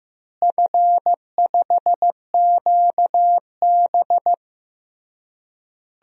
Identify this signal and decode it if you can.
Morse code, heard as a keyed tone, about 15 wpm, 700 Hz. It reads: F5QB
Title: DXCC-100 Call Signs - 4 Characters - Repeated 3x - 15wpm